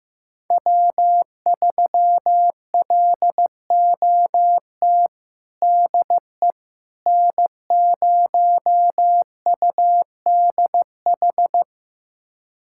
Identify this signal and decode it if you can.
Morse code, heard as a keyed tone, about 15 wpm, 700 Hz. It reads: W3LOT DE N0UDH
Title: USA Callsign DE Callsign - 5 Characters - 15wpm